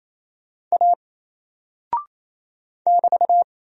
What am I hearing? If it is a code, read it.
Morse code, heard as a keyed tone, about 28 wpm, 700 Hz. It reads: A <BT>
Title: Single Character (Letter:Number:Punctuation:Prosign) Alt 28wpm